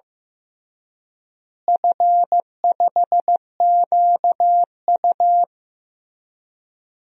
Morse code, heard as a keyed tone, about 16 wpm, 700 Hz. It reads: F5QU
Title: DXCC-100 Call Signs - 4 Characters - Repeated 3x - 15wpm